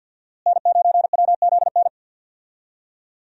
Morse code, heard as a keyed tone, about 48 wpm, 700 Hz. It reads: N9WZN
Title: USA Call Signs - 5 Characters - Repeated 3x - 50wpm